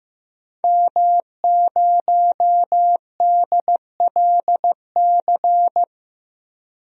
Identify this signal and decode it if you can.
Morse code, heard as a keyed tone, about 15 wpm, 700 Hz. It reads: M0DLC
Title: DXCC-100 Call Signs - 5 Characters - 15wpm